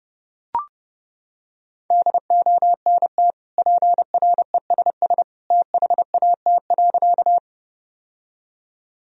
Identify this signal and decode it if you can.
Morse code, heard as a keyed tone, about 30 wpm, 700 Hz. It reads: DONT PRESS THAT.
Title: Sentences from Top 1000 Words - Review Part 1 - 30wpm